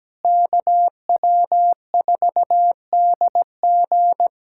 Morse code, heard as a keyed tone, about 17 wpm, 700 Hz. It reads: KW4DG
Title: USA Callsign DE Callsign - 5 Characters - 17wpm